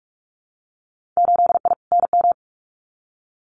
Morse code, heard as a keyed tone, about 46 wpm, 700 Hz. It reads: 8I NM
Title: Winter Field Day Exchanges Encore 45wpm